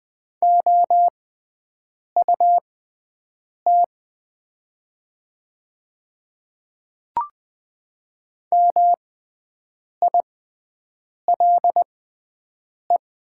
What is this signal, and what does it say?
Morse code, heard as a keyed tone, about 20 wpm, spaced as for 7 wpm, 690 Hz. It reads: OUT MILE
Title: 6x Character Spacing - Top 300 Words - 20wpm